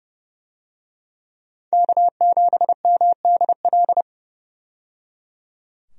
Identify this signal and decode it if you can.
Morse code, heard as a keyed tone, about 30 wpm, 700 Hz. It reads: K7MDL
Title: USA Call Signs - 5 Characters - 30wpm